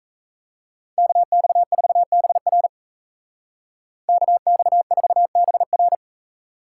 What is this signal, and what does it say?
Morse code, heard as a keyed tone, about 33 wpm, 700 Hz. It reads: KX4BR KX4BR
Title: Speed Racing - USA Call Signs 28wpm